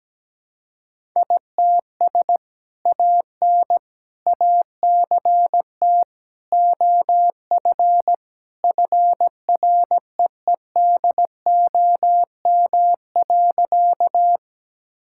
Morse code, heard as a keyed tone, about 17 wpm, 700 Hz. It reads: ITS AN ACT OF FREEDOM.
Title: Sentences from Top 1000 Words - Review Part 5 - 17wpm